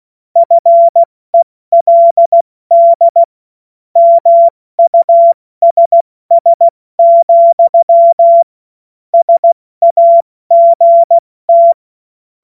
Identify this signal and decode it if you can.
Morse code, heard as a keyed tone, about 16 wpm, 680 Hz. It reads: FELD MUSS, SAGT
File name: Morsekurs / Witze / Witz 27 Mai 2025 70 F80